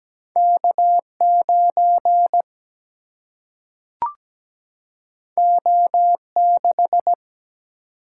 Morse code, heard as a keyed tone, about 17 wpm, 700 Hz. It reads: K9 O6
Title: Single-Letter Single-Number 17wpm